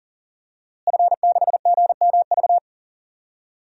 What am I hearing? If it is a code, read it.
Morse code, heard as a keyed tone, about 38 wpm, 700 Hz. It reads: F6GMV
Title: DXCC-100 Call Signs - 5 Characters - Repeated 3x - 40wpm